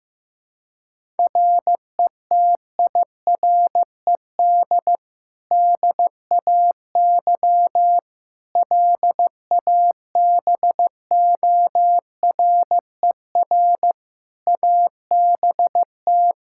Morse code, heard as a keyed tone, about 15 wpm, 700 Hz. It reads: RETIRED DAY LABORER ABT
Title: Practice QSO - Exchange 3 - More Info - 15wpm